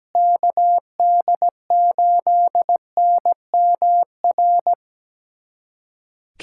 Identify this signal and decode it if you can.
Morse code, heard as a keyed tone, about 17 wpm, 700 Hz. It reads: KD8NMR
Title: USA Callsign DE Callsign - 6 Characters - 17wpm